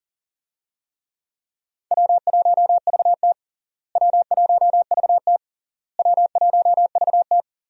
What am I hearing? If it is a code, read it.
Morse code, heard as a keyed tone, about 40 wpm, 700 Hz. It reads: W1VT W1VT W1VT
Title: USA Call Signs - 4 Characters - Repeated 3x - 40wpm